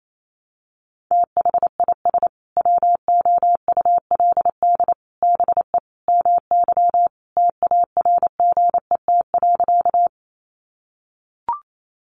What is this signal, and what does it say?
Morse code, heard as a keyed tone, about 28 wpm, 700 Hz. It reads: THIS WOULD BE MY TARGET.
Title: Sentences from Top 1200 Words 28wpm